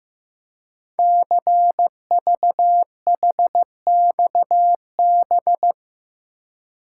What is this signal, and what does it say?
Morse code, heard as a keyed tone, about 15 wpm, 700 Hz. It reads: CVHXB
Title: Code Groups - 5 Letters - 15wpm